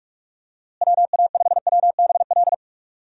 Morse code, heard as a keyed tone, about 42 wpm, 700 Hz. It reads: WA5WBL
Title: USA Call Signs - 6 Characters - Repeated 3x - 45wpm